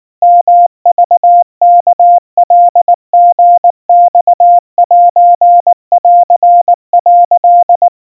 Morse code (Japanese Wabun code), heard as a keyed tone, about 19 wpm, 700 Hz. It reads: ヨクワカリマセン」
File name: kawa / Morse sample / kamyu 70 cpm 700 Hz Crispy